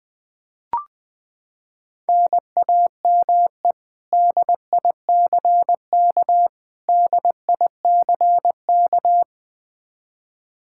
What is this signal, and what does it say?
Morse code, heard as a keyed tone, about 20 wpm, 700 Hz. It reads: NAME DICK DICK
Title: Practice QSO - Exchange 1 - Beginning - 20wpm